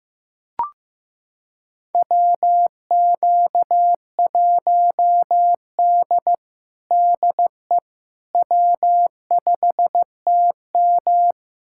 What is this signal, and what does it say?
Morse code, heard as a keyed tone, about 15 wpm, 700 Hz. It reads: WQ1D DE W5TM
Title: USA Callsign DE Callsign - 4 Characters - 15wpm